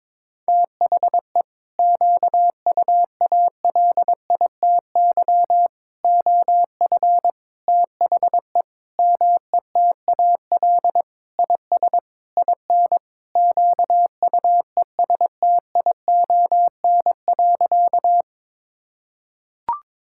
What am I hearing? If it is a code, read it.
Morse code, heard as a keyed tone, about 22 wpm, 700 Hz. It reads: THE QUALITY OF THE METAL IS IN QUESTION.
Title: Sentences from Top 1000 Words - Review Part 6 - 22wpm